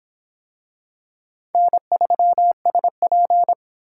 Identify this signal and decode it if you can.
Morse code, heard as a keyed tone, about 25 wpm, 700 Hz. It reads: N3SP